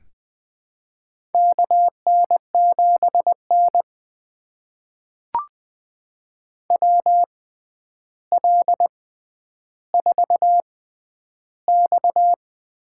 Morse code, heard as a keyed tone, about 21 wpm, spaced as for 10 wpm, 700 Hz. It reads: KN7N WL4X
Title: USA Call Signs - 4 Characters - 6x Character Spacing - 20wpm